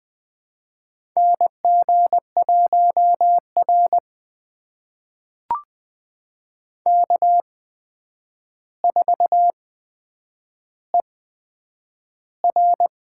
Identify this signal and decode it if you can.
Morse code, heard as a keyed tone, about 20 wpm, spaced as for 9 wpm, 700 Hz. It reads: NG1R K4ER
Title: USA Call Signs - 8x Character Spacing - 20wpm